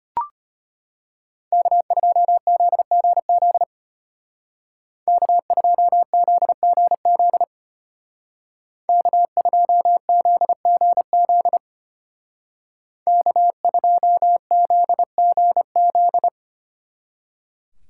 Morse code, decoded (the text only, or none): K2ZGZ K2ZGZ K2ZGZ K2ZGZ